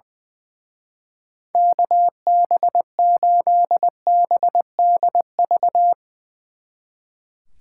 Morse code, heard as a keyed tone, about 22 wpm, 700 Hz. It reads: KB8BDV
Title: Speed Racing - USA 2x3 Call Signs 20wpm